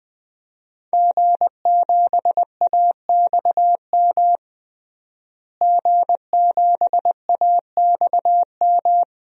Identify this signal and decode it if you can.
Morse code, heard as a keyed tone, about 20 wpm, 700 Hz. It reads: G7AXM G7AXM
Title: DXCC-100 Call Signs - 5 Characters - Repeated 2x - 20wpm